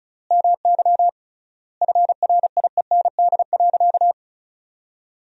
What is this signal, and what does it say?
Morse code, heard as a keyed tone, about 36 wpm, 700 Hz. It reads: MY FRIEND.
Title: 3x Word-Spacing - Sentences from Top 300 Words - 35wpm